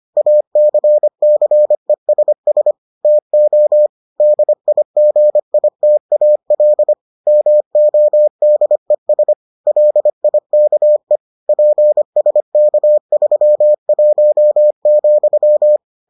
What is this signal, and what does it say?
Morse code, heard as a keyed tone, about 25 wpm, 600 Hz. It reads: ACCESS TO DIGITAL MODES LIKE PSK31,